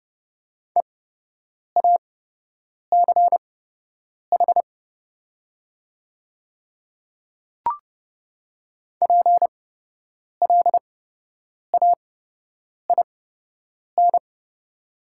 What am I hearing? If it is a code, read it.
Morse code, heard as a keyed tone, about 30 wpm, spaced as for 8 wpm, 700 Hz. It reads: EACH PLAIN
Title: 8x Character Spacing - Top 500 Words - 30wpm